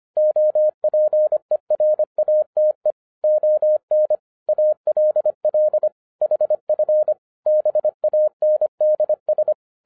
Morse code, heard as a keyed tone, about 25 wpm, 610 Hz. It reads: OPERATE ON ALL HF BANDS